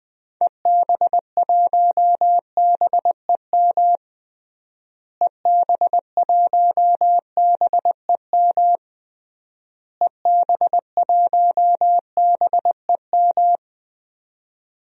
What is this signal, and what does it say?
Morse code, heard as a keyed tone, about 20 wpm, 700 Hz. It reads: EB1BEM EB1BEM EB1BEM
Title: DXCC-100 Call Signs - 6 Characters - Repeated 3x - 20wpm